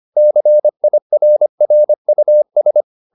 Morse code, heard as a keyed tone, about 25 wpm, 600 Hz. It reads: CIRRUS
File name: the old man and the sea part 1 25wpm